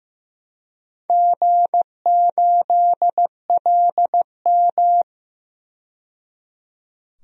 Morse code, heard as a keyed tone, about 16 wpm, 700 Hz. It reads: G8LM
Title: DXCC-100 Call Signs - 4 Characters - 15wpm